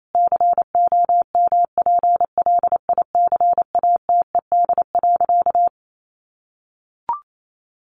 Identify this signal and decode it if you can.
Morse code, heard as a keyed tone, about 28 wpm, 700 Hz. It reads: COMPLICATED.
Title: Sentences from Top 1200 Words 28wpm